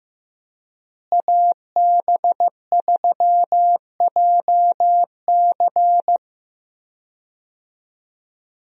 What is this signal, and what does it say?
Morse code, heard as a keyed tone, about 15 wpm, 700 Hz. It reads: AB3JC
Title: USA Call Signs - 5 Characters - Repeated 2x - 15wpm